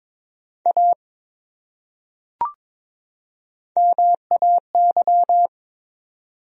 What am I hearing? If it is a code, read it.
Morse code, heard as a keyed tone, about 22 wpm, 700 Hz. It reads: A MAY
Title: Top 100 Words 22wpm